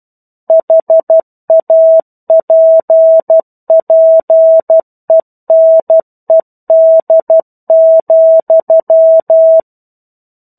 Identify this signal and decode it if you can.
Morse code, heard as a keyed tone, about 12 wpm, 650 Hz. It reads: HAPPENED,
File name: 01-Chapter-01 12 WPM